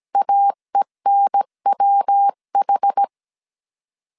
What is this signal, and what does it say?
Morse code, heard as a keyed tone, about 17 wpm, 800 Hz. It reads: AENWH
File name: CW / DET / 1 3 1500C S A 16wpm 2141